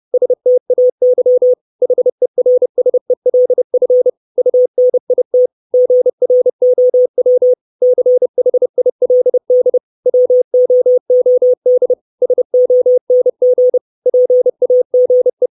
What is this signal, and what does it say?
Morse code, heard as a keyed tone, about 30 wpm, 500 Hz. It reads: STAY HERSELF UNIT GROW CHILD WOOD SONG PAGE
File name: 632 Words 30WPM